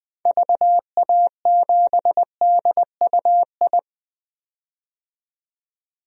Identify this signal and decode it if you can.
Morse code, heard as a keyed tone, about 20 wpm, 700 Hz. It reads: VA7DUI
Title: DXCC-100 Call Signs - 6 Characters - 20wpm